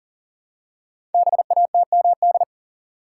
Morse code, heard as a keyed tone, about 43 wpm, 700 Hz. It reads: BATMD
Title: Code Groups - 5 Letters - 40wpm